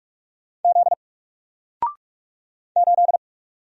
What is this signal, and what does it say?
Morse code, heard as a keyed tone, about 44 wpm, 700 Hz. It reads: Z 8
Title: Single Letter-Number Alt 45wpm